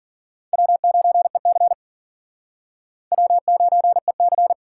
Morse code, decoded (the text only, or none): W9EC W9EC